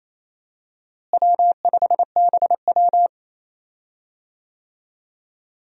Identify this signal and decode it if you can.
Morse code, heard as a keyed tone, about 27 wpm, 700 Hz. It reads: W5BW
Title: USA Call Signs - 4 Characters - Repeated 2x - 28wpm